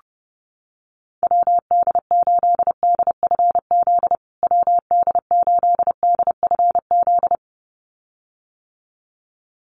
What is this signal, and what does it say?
Morse code, heard as a keyed tone, about 30 wpm, 700 Hz. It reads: WD8DFZ WD8DFZ
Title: USA Call Signs Repeated 2x 30wpm